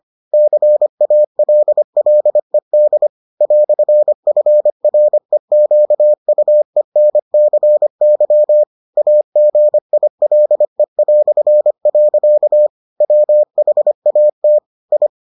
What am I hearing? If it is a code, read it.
Morse code, heard as a keyed tone, about 25 wpm, 600 Hz. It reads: CALLED "FREQUENCY AGILE". WHAT I